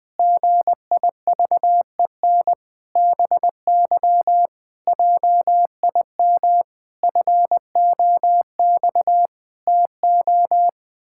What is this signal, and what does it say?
Morse code, heard as a keyed tone, about 20 wpm, 700 Hz. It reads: GIVEN BY JIM FOX TO
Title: Pangram Challenge v2 - 20wpm